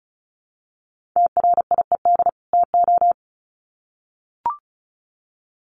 Morse code, heard as a keyed tone, about 34 wpm, 700 Hz. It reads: TRIED TO
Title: Sets of 2 Words 35wpm